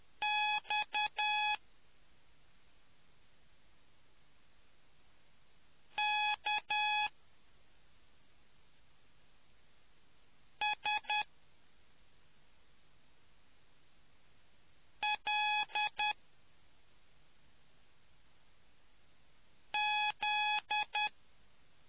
Morse code, heard as a keyed tone, about 10 wpm, spaced as for 2 wpm, 840 Hz. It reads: XKSLZ